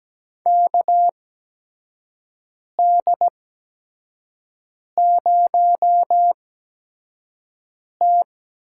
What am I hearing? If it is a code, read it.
Morse code, heard as a keyed tone, about 17 wpm, spaced as for 5 wpm, 700 Hz. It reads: KD0T
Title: USA Call Signs - 8x Character Spacing - 17wpm